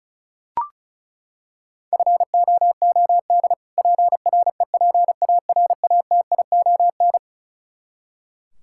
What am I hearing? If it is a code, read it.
Morse code, heard as a keyed tone, about 34 wpm, 700 Hz. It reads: FOOD PREPARATION
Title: Practice QSO - Exchange 3 - More Info - 35wpm